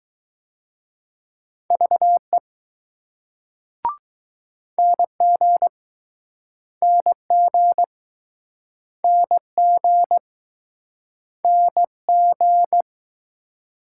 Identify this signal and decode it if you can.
Morse code, heard as a keyed tone, about 22 wpm, 700 Hz. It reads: VE NG NG NG NG